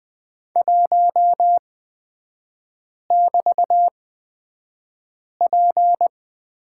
Morse code, heard as a keyed tone, about 20 wpm, spaced as for 5 wpm, 700 Hz. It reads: 1<BT>P
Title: 224 Lesson 36e - BT - Three Characters 20wpm